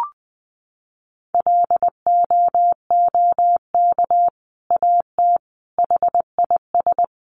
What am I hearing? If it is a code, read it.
Morse code, heard as a keyed tone, about 20 wpm, 700 Hz. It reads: LOOK AT HIS
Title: Sentences from Top 1500 Words 20wpm